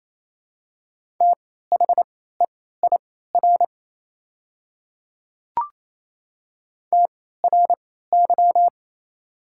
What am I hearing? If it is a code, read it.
Morse code, heard as a keyed tone, about 28 wpm, spaced as for 16 wpm, 700 Hz. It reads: THEIR TRY
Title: 3x Character Spacing - Top 200 Words - 28wpm